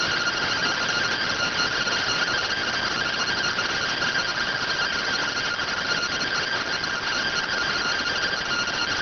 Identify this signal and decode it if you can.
Two-tone teleprinter signal, about 170 Hz shift.